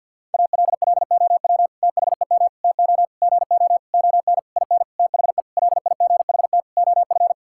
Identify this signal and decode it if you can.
Morse code, heard as a keyed tone, about 50 wpm, 700 Hz. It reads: ALLOW THEM TO GO ON IN THE LIGHT OF
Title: Sentences from Top 1000 Words - Review Part 5 - 50wpm